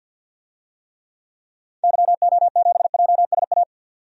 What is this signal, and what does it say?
Morse code, heard as a keyed tone, about 52 wpm, 700 Hz. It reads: YO7JSA
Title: DXCC-100 Call Signs - 6 Characters - 50wpm